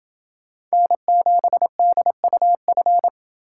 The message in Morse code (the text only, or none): N7DUF